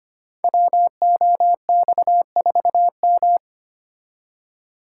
Morse code, heard as a keyed tone, about 25 wpm, 700 Hz. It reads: WOX4M
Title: Code Groups - 5 Letter:Numbers - 25wpm